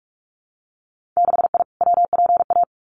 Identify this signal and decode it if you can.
Morse code, heard as a keyed tone, about 46 wpm, 700 Hz. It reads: 6I WPA
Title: Winter Field Day Exchanges Encore 45wpm